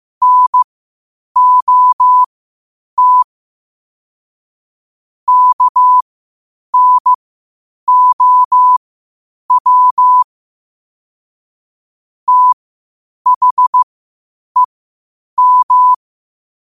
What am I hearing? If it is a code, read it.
Morse code, heard as a keyed tone, about 15 wpm, spaced as for 8 wpm, 1000 Hz. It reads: NOT KNOW THEM
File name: qotdfiles / 8wpm